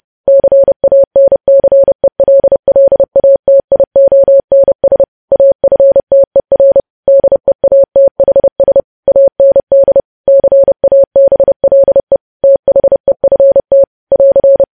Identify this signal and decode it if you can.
Morse code, heard as a keyed tone, about 30 wpm, 550 Hz. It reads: CANCELLATIONS AFTER DEATHS AND CABLE THEFT <AR>
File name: podcast / News headlines-1x-30x30